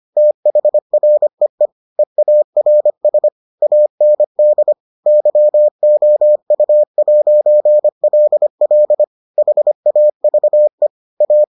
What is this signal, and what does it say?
Morse code, heard as a keyed tone, about 25 wpm, 600 Hz. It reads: THREE EARS AND YOU'LL HAVE A